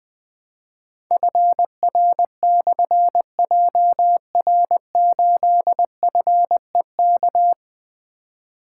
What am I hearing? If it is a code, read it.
Morse code, heard as a keyed tone, about 21 wpm, 700 Hz. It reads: FR/JR8FEK
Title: Difficult Call Signs 20wpm